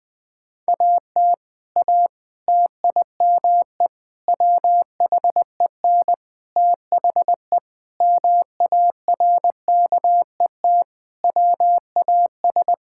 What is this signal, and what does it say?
Morse code, heard as a keyed tone, about 21 wpm, 700 Hz. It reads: AT A TIME WHEN THE MARKET WAS
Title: Sentences with 5-Word Phrases - 20wpm